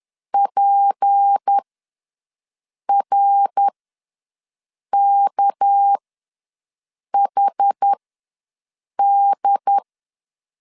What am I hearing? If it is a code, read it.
Morse code, heard as a keyed tone, about 10 wpm, spaced as for 5 wpm, 800 Hz. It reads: PRKHD
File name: LEZIONI / 6 PKX0 / 6 PK ws 10wpm 3745